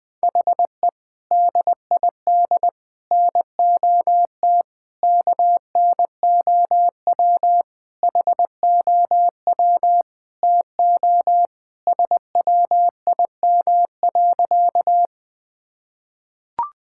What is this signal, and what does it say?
Morse code, heard as a keyed tone, about 20 wpm, 700 Hz. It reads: HE DID NOT KNOW HOW TO SWIM.
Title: Sentences with 5-Word Phrases - 20wpm